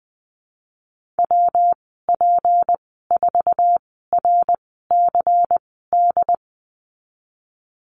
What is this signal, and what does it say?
Morse code, heard as a keyed tone, about 20 wpm, spaced as for 15 wpm, 700 Hz. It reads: WP4RCD
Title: 167 Lesson 27g - Number 7 - Call Signs 20wpm